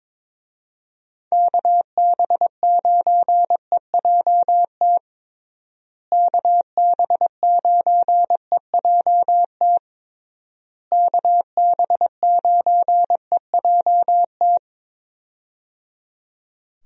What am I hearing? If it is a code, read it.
Morse code, heard as a keyed tone, about 22 wpm, 700 Hz. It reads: KB9EJT KB9EJT KB9EJT
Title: USA Call Signs - 6 Characters - Repeated 3x - 22wpm